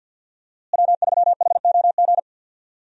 Morse code, heard as a keyed tone, about 52 wpm, 700 Hz. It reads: W3HOG
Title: USA Call Signs - 5 Characters - Repeated 3x - 50wpm